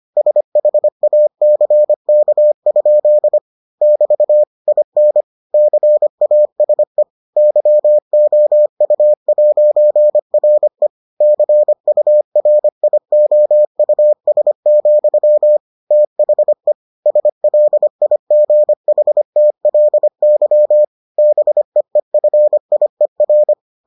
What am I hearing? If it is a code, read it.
Morse code, heard as a keyed tone, about 25 wpm, 600 Hz. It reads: SHACK? <BT> IN CASE YOU'RE CURIOUS, THE SLIGHTLY BEEFIER